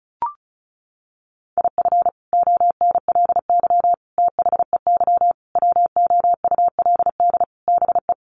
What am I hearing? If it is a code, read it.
Morse code, heard as a keyed tone, about 34 wpm, 700 Hz. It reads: IF ONLY THEY WOULD BE
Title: Sentences from Top 500 Words 35wpm